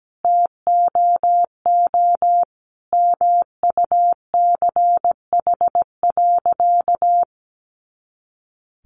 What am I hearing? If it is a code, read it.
Morse code, heard as a keyed tone, about 17 wpm, 700 Hz. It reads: TOO MUCH.
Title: Sentences from Top 300 Words 17wpm